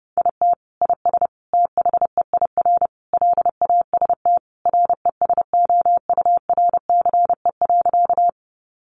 Morse code, heard as a keyed tone, about 30 wpm, 700 Hz. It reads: IT IS THEIR LAST RESOURCE.
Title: Sentences from Top 1300 Words 30wpm